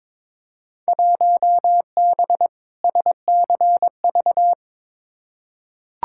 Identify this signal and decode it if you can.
Morse code, heard as a keyed tone, about 22 wpm, 700 Hz. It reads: 1B SCV
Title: Field Day Exchanges Encore - 22wpm